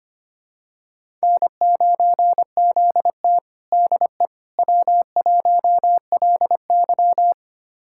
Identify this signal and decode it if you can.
Morse code, heard as a keyed tone, about 25 wpm, 700 Hz. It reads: N9ZT DE W1LY
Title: USA Callsign DE Callsign - 4 Characters - 25wpm